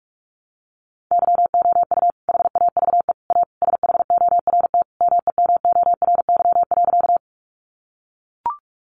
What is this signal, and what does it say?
Morse code, heard as a keyed tone, about 44 wpm, 700 Hz. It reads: YOU HAVE A SHORT MEMORY.
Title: Sentences from Top 1300 Words 45wpm